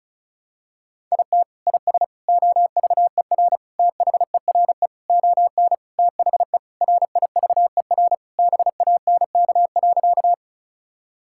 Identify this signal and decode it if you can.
Morse code, heard as a keyed tone, about 36 wpm, 700 Hz. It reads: IT IS OVER THERE ON THE RIVER BANK.
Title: Sentences from Top 1000 Words - Review Part 6 - 35wpm